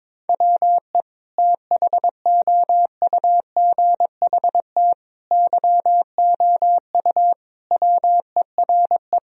Morse code, heard as a keyed tone, about 22 wpm, 700 Hz. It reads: WE THOUGHT YOU WERE
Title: Sentences from Top 1000 Words - Review Part 4 - 22wpm